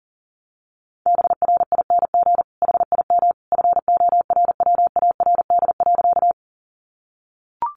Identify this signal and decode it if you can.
Morse code, heard as a keyed tone, about 40 wpm, 700 Hz. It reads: BRING HIM FORWARD.
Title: Sentences from Top 1300 Words 40wpm